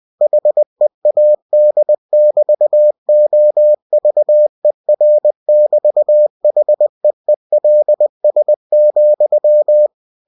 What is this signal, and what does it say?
Morse code, heard as a keyed tone, about 20 wpm, 600 Hz. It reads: HEAD<BT>OVER<BT>HEELS,
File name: cw-books / winnie-the-pooh / winnie-20wpm-page-0003